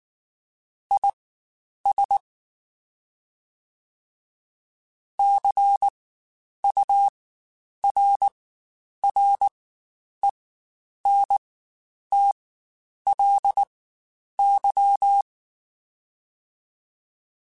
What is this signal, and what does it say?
Morse code, heard as a keyed tone, about 19 wpm, spaced as for 9 wpm, 800 Hz. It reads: IS CURRENTLY
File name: Spaced Characters / Spaced Characters-mp3s / Weather5 5 sp @18